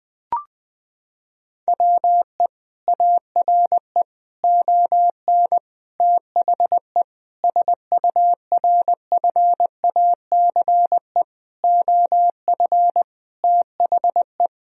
Morse code, heard as a keyed tone, about 20 wpm, 700 Hz. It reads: WE ARE ON THE SURFACE OF THE
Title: Sentences from Top 1000 Words - Review Part 3 - 20wpm